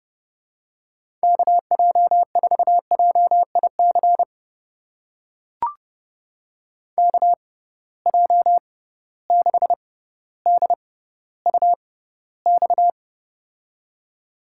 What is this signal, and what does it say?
Morse code, heard as a keyed tone, about 31 wpm, spaced as for 15 wpm, 700 Hz. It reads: KJ4JIC KJ6DUX
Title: USA Call Signs - 6 Characters - 6x Character Spacing - 30wpm